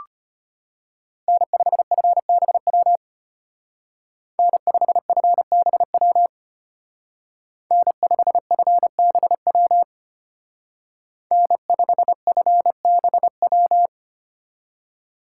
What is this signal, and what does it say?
Morse code, heard as a keyed tone, about 30 wpm, 700 Hz. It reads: N5FBW N5FBW N5FBW N5FBW
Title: Speed Racing - USA 1x3 Call Signs 25wpm